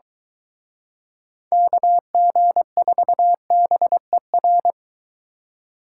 Morse code, heard as a keyed tone, about 23 wpm, 700 Hz. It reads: KG4BER